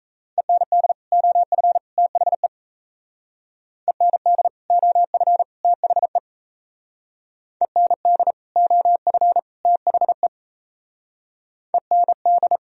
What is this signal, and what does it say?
Morse code, heard as a keyed tone, about 33 wpm, 700 Hz. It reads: END OF THE END OF THE END OF THE END
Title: Speed-Racing - Sets of 3 Words 28wpm